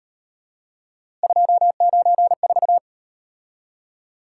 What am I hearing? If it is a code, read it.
Morse code, heard as a keyed tone, about 38 wpm, 700 Hz. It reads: 294